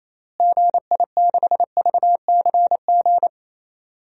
Morse code, heard as a keyed tone, about 28 wpm, 700 Hz. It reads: GI6VCG
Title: DXCC-100 Call Signs - 6 Characters - Repeated 2x - 28wpm